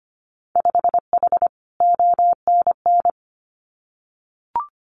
Morse code, heard as a keyed tone, about 25 wpm, 700 Hz. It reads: 5H ONN
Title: Winter Field Day Exchanges Encore 25wpm